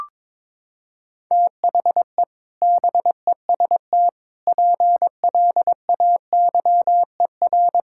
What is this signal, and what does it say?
Morse code, heard as a keyed tone, about 22 wpm, 700 Hz. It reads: THE BEST PLAYER
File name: practice / 22wpm / Sets of 6 Words - 22wpm